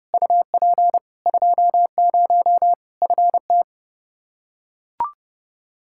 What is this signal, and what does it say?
Morse code, heard as a keyed tone, about 32 wpm, 700 Hz. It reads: UP 20 FT
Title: K1USN SST QSO Practice 30wpm